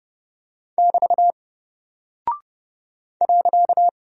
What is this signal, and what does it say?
Morse code, heard as a keyed tone, about 30 wpm, 700 Hz. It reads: <BT> .
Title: Single Character (Letter:Number:Punctuation:Prosign) 30wpm